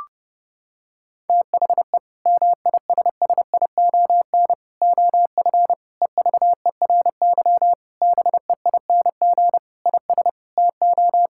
Morse code, heard as a keyed tone, about 30 wpm, 700 Hz. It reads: THE MISSION OF EVERY BEING IS TO
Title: Sentences from Top 1600 Words 30wpm